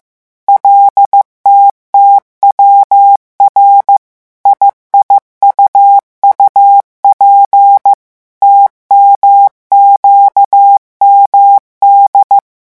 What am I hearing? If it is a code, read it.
Morse code, heard as a keyed tone, about 15 wpm, 800 Hz. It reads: LTTWR IIUUP TMQMD
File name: morse mp3 / revision7-vitesse 14 mots